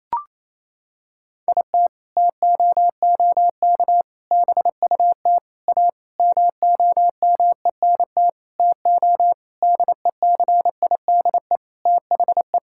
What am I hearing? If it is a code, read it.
Morse code, heard as a keyed tone, about 28 wpm, 700 Hz. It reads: IT TOOK BUT A MOMENT TO DECIDE THE
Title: Sentences from Top 1600 Words 28wpm